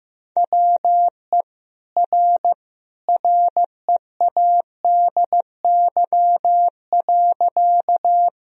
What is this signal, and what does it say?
Morse code, heard as a keyed tone, about 15 wpm, 700 Hz. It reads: WE R READY.
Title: Sentences from Top 400 Words and Abbreviations 15wpm